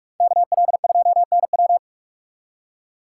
Morse code, heard as a keyed tone, about 42 wpm, 700 Hz. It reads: KL2NW